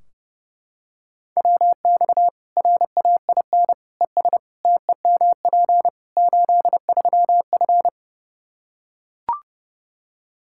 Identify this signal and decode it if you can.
Morse code, heard as a keyed tone, about 30 wpm, 700 Hz. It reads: WX RAIN ES TEMP 83F
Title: Practice QSO - Exchange 2 - Info - 30wpm